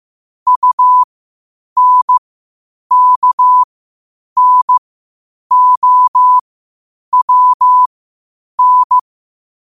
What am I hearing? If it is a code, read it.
Morse code, heard as a keyed tone, about 15 wpm, spaced as for 8 wpm, 1000 Hz. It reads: UNKNOWN
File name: qotdfiles / 8wpm